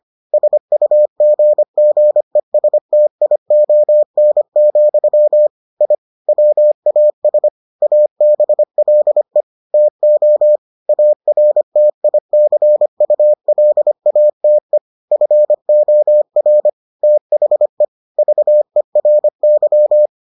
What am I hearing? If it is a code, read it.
Morse code, heard as a keyed tone, about 25 wpm, 600 Hz. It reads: SUGGESTION, I WAS ABLE TO ARTICULATE FOR THE VERY